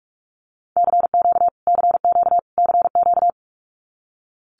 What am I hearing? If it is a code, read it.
Morse code, heard as a keyed tone, about 44 wpm, 700 Hz. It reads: CQ CQ CQ
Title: Abbreviations Repeated 3x - 45wpm